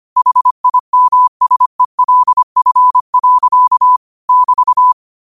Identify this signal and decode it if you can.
Morse code, heard as a keyed tone, about 25 wpm, 1000 Hz. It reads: HIMSELF. <BT>
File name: qotdfiles / 25wpm